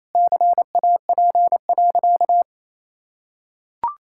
Morse code, heard as a keyed tone, about 28 wpm, 700 Hz. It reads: CAP.
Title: Sentences from Top 1600 Words 28wpm